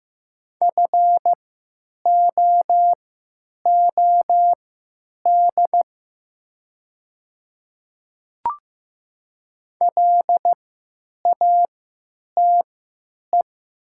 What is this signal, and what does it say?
Morse code, heard as a keyed tone, about 15 wpm, spaced as for 9 wpm, 700 Hz. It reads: FOOD LATE
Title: 3x Character Spacing - Top 500 Words - 15wpm